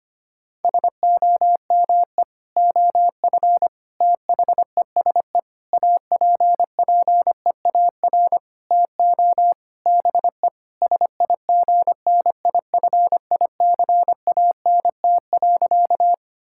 Morse code, heard as a keyed tone, about 25 wpm, 700 Hz. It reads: SOME OF THESE APPEAR TO BE SIGNIFICANT.
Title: Sentences from Top 1000 Words - Review Part 6 - 25wpm